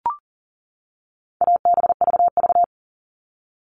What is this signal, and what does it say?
Morse code, heard as a keyed tone, about 40 wpm, 700 Hz. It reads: ABVV
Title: All Cut Numbers - 4-Digit 40wpm